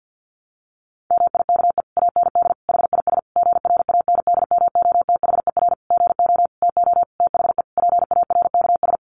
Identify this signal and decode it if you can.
Morse code, heard as a keyed tone, about 50 wpm, 700 Hz. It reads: MIKE AND HIS GRANDMOTHER GO TO THE PARKS
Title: Additional Easy Phrases 50wpm